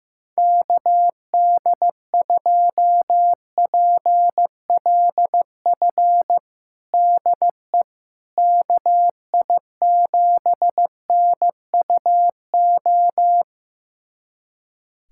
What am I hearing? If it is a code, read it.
Morse code, heard as a keyed tone, about 15 wpm, 700 Hz. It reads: KD2PLF DE KI7NUO
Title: USA Callsign DE Callsign - 6 Characters - 15wpm